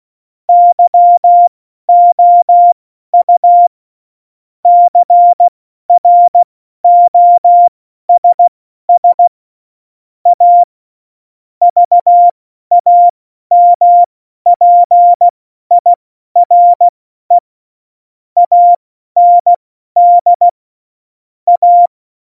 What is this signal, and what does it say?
Morse code, heard as a keyed tone, about 16 wpm, spaced as for 12 wpm, 700 Hz. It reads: YOU CROSS A VAMPIRE AND A
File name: Morsekurs / Witze